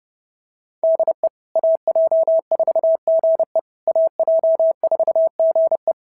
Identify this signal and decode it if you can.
Morse code, heard as a keyed tone, about 32 wpm, 690 Hz. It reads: DE AJ4GE AJ4GE
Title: SOTA QSO Practice 30wpm